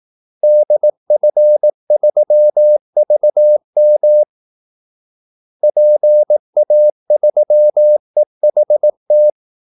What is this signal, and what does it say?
Morse code, heard as a keyed tone, about 18 wpm, 600 Hz. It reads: DF3VM PA3EHT
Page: Kaldesignaler 18 wpm